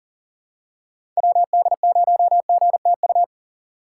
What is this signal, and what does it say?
Morse code, heard as a keyed tone, about 40 wpm, 700 Hz. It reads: WD0GTU
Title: USA Call Signs - 6 Characters - 6x Character Spacing - 40wpm